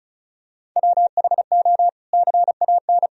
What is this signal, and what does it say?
Morse code, heard as a keyed tone, about 36 wpm, 700 Hz. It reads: WHO CAN
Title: Sentences from Top 300 Words and Abbreviations 35wpm